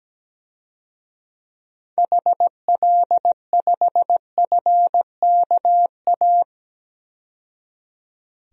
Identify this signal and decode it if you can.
Morse code, heard as a keyed tone, about 17 wpm, 700 Hz. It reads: HL5FKA
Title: DXCC-100 Call Signs - 6 Characters - 17wpm